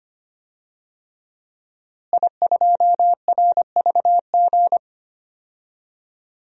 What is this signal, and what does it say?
Morse code, heard as a keyed tone, about 25 wpm, 700 Hz. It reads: I2RVG
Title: DXCC-100 Call Signs - 5 Characters - Repeated 3x - 25wpm